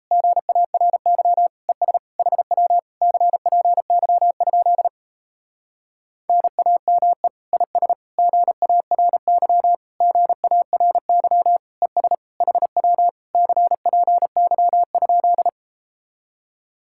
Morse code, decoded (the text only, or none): GARY ES HW CPY? NAME IS GARY GARY ES HW CPY?